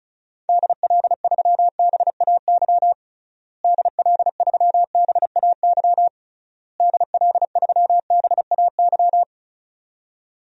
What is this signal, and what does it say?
Morse code, heard as a keyed tone, about 36 wpm, 700 Hz. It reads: DL3BAY DL3BAY DL3BAY
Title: DXCC-100 Call Signs - 6 Characters - Repeated 3x - 35wpm